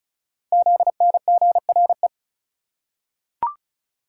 Morse code, heard as a keyed tone, about 36 wpm, 690 Hz. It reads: ZNGRE
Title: Code Groups - 5 Letters - 35wpm